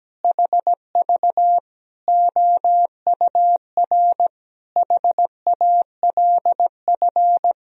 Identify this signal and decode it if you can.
Morse code, heard as a keyed tone, about 17 wpm, 700 Hz. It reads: HV OUR HALF
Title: Sentences from Top 400 Words and Abbreviations 17wpm